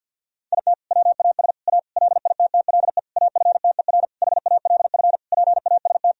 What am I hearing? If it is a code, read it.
Morse code, heard as a keyed tone, about 48 wpm, 700 Hz. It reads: IT WAS A LITTLE AFTER HALF PAST
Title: Sentences from Top 1000 Words - Review Part 3 - 50wpm